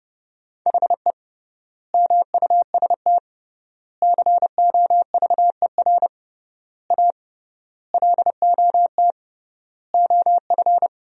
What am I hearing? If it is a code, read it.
Morse code, heard as a keyed tone, about 29 wpm, 700 Hz. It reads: HE MUST COVER A LOT OF
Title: 3x Word-Spacing - Sentences from Top 500 Words - 30wpm